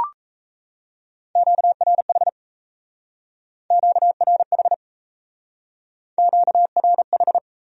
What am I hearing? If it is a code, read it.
Morse code, heard as a keyed tone, about 35 wpm, 700 Hz. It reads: QRH QRH QRH
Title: Speed-Racing Q-Codes - 28wpm